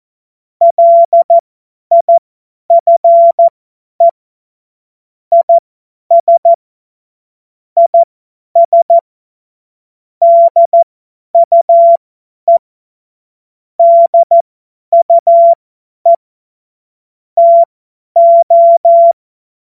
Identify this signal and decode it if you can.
Morse code, heard as a keyed tone, about 14 wpm, spaced as for 10 wpm, 680 Hz. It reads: LIFE IS IS DUE DUE TO
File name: Morsekurs / Witze / Witz 14. Dezember 2021, 50 F70